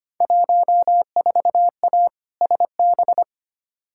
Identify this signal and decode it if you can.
Morse code, heard as a keyed tone, about 25 wpm, 700 Hz. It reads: 14A SB
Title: Field Day Exchanges Encore - 25wpm